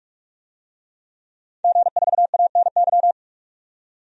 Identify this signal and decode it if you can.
Morse code, heard as a keyed tone, about 42 wpm, 700 Hz. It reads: G4ANY
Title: DXCC-100 Call Signs - 5 Characters - 45wpm